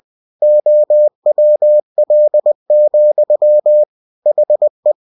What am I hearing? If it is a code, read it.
Morse code, heard as a keyed tone, about 20 wpm, 600 Hz. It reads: OWL, HE
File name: cw-books / winnie-the-pooh / winnie-20wpm-page-0050